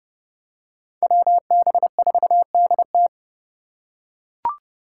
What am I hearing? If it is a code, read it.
Morse code, heard as a keyed tone, about 30 wpm, 700 Hz. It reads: WB4DT
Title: USA Call Signs - 5 Characters - 3x Character Spacing - 30wpm